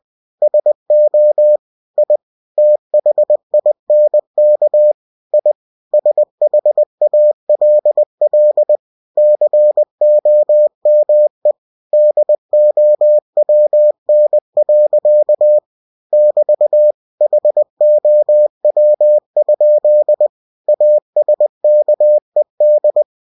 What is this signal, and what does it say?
Morse code, heard as a keyed tone, about 20 wpm, 600 Hz. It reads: SO I THINK I SHALL COME DOWN. <BT> HOW? ASKED
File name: cw-books / winnie-the-pooh / winnie-20wpm-page-0009